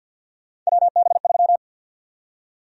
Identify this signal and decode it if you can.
Morse code, heard as a keyed tone, about 52 wpm, 700 Hz. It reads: WB3
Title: USA Call Sign Prefixes - 50wpm